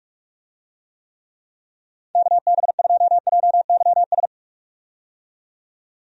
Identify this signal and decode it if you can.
Morse code, heard as a keyed tone, about 46 wpm, 700 Hz. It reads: KB2JYS
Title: USA Call Signs - 6 Characters - 45wpm